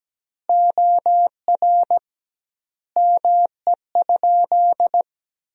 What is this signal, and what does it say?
Morse code, heard as a keyed tone, about 17 wpm, 700 Hz. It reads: OR ME?
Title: 2x Word-Spacing - Sentences from Top 200 Words - 17wpm